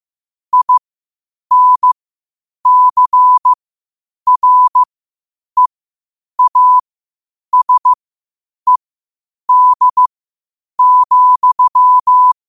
Morse code, heard as a keyed tone, about 15 wpm, spaced as for 8 wpm, 1000 Hz. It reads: INCREASED,
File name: qotdfiles / 8wpm